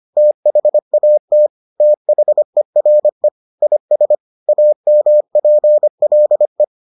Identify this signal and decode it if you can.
Morse code, heard as a keyed tone, about 25 wpm, 600 Hz. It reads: THAT THERE IS AMPLE